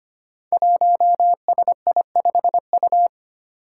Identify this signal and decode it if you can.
Morse code, heard as a keyed tone, about 25 wpm, 700 Hz. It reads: 1SI5U
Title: Five Challenging Letter-Number Combinations 25wpm